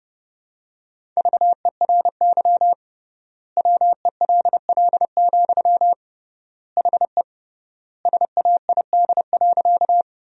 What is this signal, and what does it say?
Morse code, heard as a keyed tone, about 32 wpm, 700 Hz. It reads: VERY WELL, HE SAID.
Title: 3x Word-Spacing - Sentences from Top 200 Words - 30wpm